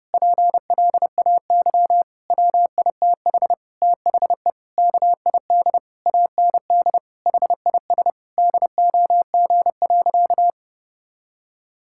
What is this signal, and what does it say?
Morse code, heard as a keyed tone, about 29 wpm, 700 Hz. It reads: PLAY WITH THE KID AND HIS DOG.
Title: Sentences from Top 1600 Words 30wpm